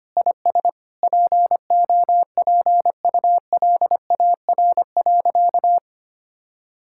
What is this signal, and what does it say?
Morse code, heard as a keyed tone, about 25 wpm, 700 Hz. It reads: IS POPULAR.
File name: practice / 25wpm / Sentences with 3-Word Phrases - 25wpm